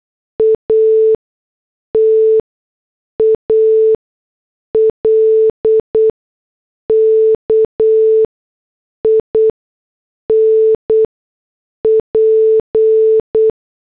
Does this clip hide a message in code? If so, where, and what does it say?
Yes: Morse code from 0.4 to 13.5 s, heard as a keyed tone, about 8 wpm, spaced as for 6 wpm, 440 Hz. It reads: A TALK IN P